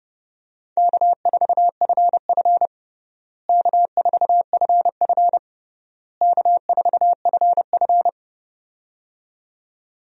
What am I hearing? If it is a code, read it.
Morse code, heard as a keyed tone, about 29 wpm, 700 Hz. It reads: K4FF K4FF K4FF
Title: USA Call Signs - 4 Characters - Repeated 3x - 30wpm